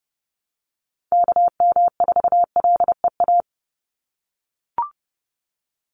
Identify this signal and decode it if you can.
Morse code, heard as a keyed tone, about 30 wpm, 700 Hz. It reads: KM4LEA
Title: USA Call Signs 30wpm